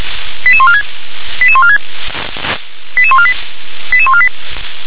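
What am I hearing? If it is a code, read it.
Five-tone selective call (ZVEI), heard as quick runs of five tones, tones of about 70 ms. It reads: 88146 88146 88148 88148
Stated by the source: ZVEI five-tone selective call received by SDR.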